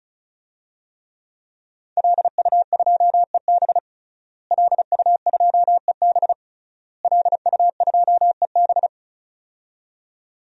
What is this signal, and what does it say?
Morse code, heard as a keyed tone, about 34 wpm, 700 Hz. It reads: LU2EB LU2EB LU2EB
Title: DXCC-100 Call Signs - 5 Characters - Repeated 3x - 35wpm